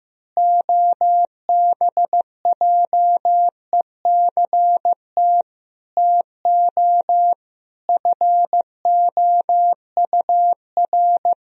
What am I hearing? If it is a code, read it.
Morse code, heard as a keyed tone, about 15 wpm, 700 Hz. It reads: OBJECT TO FOUR
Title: Sentences from Top 1000 Words - Review Part 3 - 15wpm